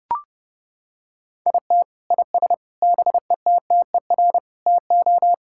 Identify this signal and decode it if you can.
Morse code, heard as a keyed tone, about 30 wpm, 700 Hz. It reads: IT IS BETTER TO
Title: Sentences from Top 1600 Words 30wpm